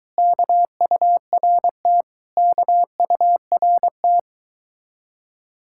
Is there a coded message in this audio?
Yes: Morse code from 0.2 to 4.2 s, heard as a keyed tone, about 23 wpm, 700 Hz. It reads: KURT KURT